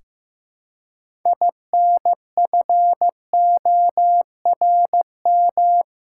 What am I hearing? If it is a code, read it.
Morse code, heard as a keyed tone, about 15 wpm, 700 Hz. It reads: INFORM
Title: Six Letter Words - 15wpm